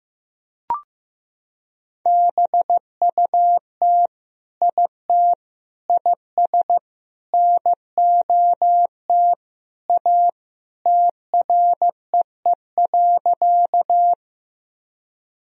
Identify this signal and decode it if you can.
Morse code, heard as a keyed tone, about 16 wpm, 700 Hz. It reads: BUT IT IS NOT A TREE.
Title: Sentences from Top 1000 Words - Review Part 1 - 15wpm